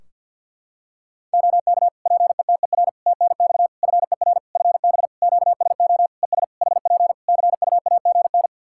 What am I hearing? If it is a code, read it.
Morse code, heard as a keyed tone, about 48 wpm, 700 Hz. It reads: OK PETER TNX FER FB QSO ES HP CUAGN
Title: K1USN SST QSO Practice 50wpm